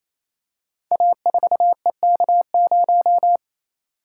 Morse code, heard as a keyed tone, about 27 wpm, 700 Hz. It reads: A4EK0
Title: Code Groups - 5 Letter:Numbers - 28wpm